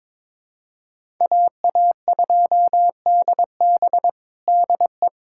Morse code, heard as a keyed tone, about 22 wpm, 700 Hz. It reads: AA2DB DE
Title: USA Callsign DE Callsign - 5 Characters - 22wpm